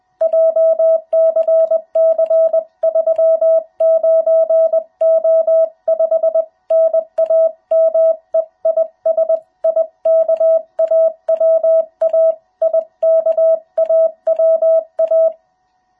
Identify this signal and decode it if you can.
Morse code, heard as a keyed tone, about 20 wpm, 660 Hz. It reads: JCC39O5 NAME IS IKAWA IKAWA
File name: jh8cbh / 2021cw